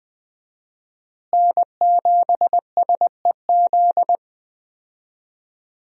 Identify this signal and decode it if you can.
Morse code, heard as a keyed tone, about 20 wpm, 700 Hz. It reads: N7SEZ
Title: USA Call Signs - 5 Characters - Repeated 2x - 20wpm